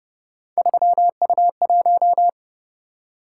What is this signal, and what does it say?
Morse code, heard as a keyed tone, about 27 wpm, 700 Hz. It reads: 3U1